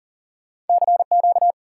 Morse code, heard as a keyed tone, about 57 wpm, spaced as for 48 wpm, 700 Hz. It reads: CQ
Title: Speed-Racing Common QSO Phrases - 40wpm